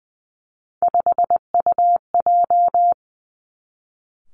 Morse code, heard as a keyed tone, about 20 wpm, 700 Hz. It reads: 5UJ
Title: Three Challenging Letter-Number Combinations 20wpm